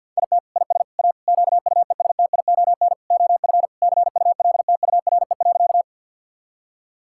Morse code, heard as a keyed tone, about 50 wpm, 700 Hz. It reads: IT IS A QUESTION OF CULTURE.
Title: Sentences from Top 1000 Words - Review Part 6 - 50wpm